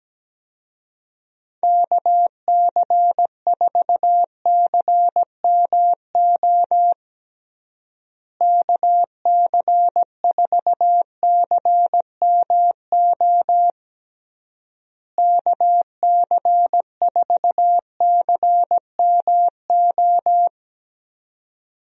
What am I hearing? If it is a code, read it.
Morse code, heard as a keyed tone, about 17 wpm, 700 Hz. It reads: KC4CMO KC4CMO KC4CMO
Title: USA Call Signs - 6 Characters - Repeated 3x - 17wpm